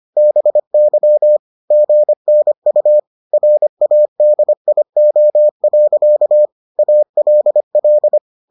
Morse code, heard as a keyed tone, about 25 wpm, 600 Hz. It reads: BY GNU RADIO. ALL